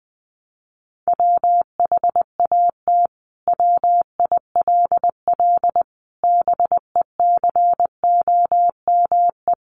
Morse code, heard as a keyed tone, about 20 wpm, 700 Hz. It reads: WHAT WILL BECOME
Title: Sentences from Top 600 Words 20wpm